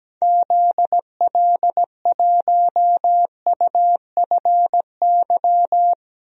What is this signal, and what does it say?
Morse code, heard as a keyed tone, about 17 wpm, 700 Hz. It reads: ZL1UFY
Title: DXCC-100 Call Signs - 6 Characters - Repeated 2x - 17wpm